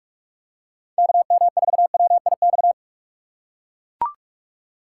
Morse code, heard as a keyed tone, about 44 wpm, 700 Hz. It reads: KM4WIX
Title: USA Call Signs - 6 Characters - 8x Character Spacing - 45wpm